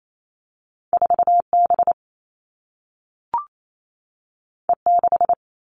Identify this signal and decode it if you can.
Morse code, heard as a keyed tone, about 28 wpm, 700 Hz. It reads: 4B E6
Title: All Cut Numbers - 2-Digit 28wpm